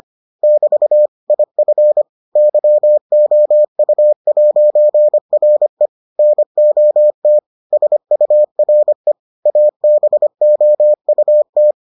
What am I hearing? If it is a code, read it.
Morse code, heard as a keyed tone, about 25 wpm, 600 Hz. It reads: <BT> IF YOU'RE NOT SURE ABOUT